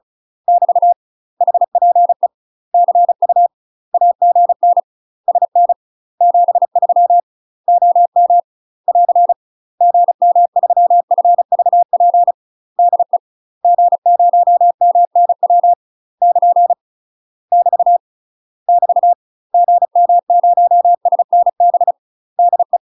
Morse code, heard as a keyed tone, about 36 wpm, 700 Hz. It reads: <BT> HPE CU AGN SN 73 OM <AR> GM3FVP DE G0MNW <KN> <BT> <BT> GM0SNB DE
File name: UK MASTER QSOs TRAINING / 35WPM / UK MASTER QSOs 0036